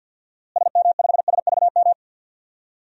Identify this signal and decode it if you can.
Morse code, heard as a keyed tone, about 51 wpm, 700 Hz. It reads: SM5SVM
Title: DXCC-100 Call Signs - 6 Characters - Repeated 2x - 50wpm